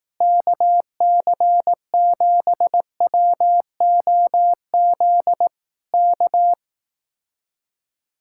Morse code, heard as a keyed tone, about 18 wpm, 700 Hz. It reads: KC7WOZ K
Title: Speed-Racing Common QSO Phrases Encore - 15wpm